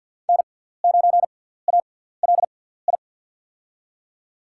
Morse code, heard as a keyed tone, about 48 wpm, spaced as for 17 wpm, 700 Hz. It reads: N9ALI
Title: USA Call Signs - 5 Characters - 6x Character Spacing - 50wpm